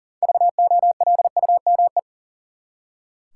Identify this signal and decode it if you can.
Morse code, heard as a keyed tone, about 40 wpm, 700 Hz. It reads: VOLUME
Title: Six Letter Words - 40wpm